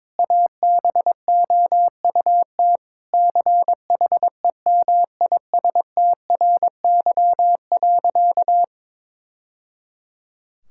Morse code, heard as a keyed tone, about 22 wpm, 700 Hz. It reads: ABOUT CHEMISTRY.
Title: Sentences from Top 1000 Words - Review Part 6 - 22wpm